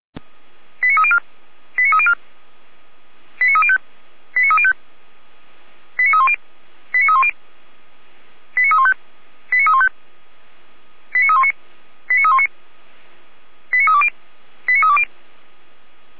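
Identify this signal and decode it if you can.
Five-tone selective call (ZVEI), heard as quick runs of five tones, tones of about 70 ms. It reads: 89304 89304 89305 89305 89310 89310 89315 89315 89319 89319 89320 89320
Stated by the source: ZVEI five-tone call received on SDR.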